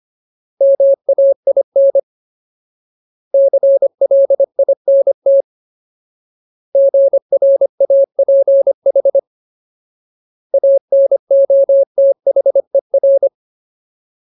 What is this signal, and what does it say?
Morse code, heard as a keyed tone, about 25 wpm, 550 Hz. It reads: MAIN CLINT GRAPH ANOTHER
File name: Ultimate Word List / 25WPM Ultimate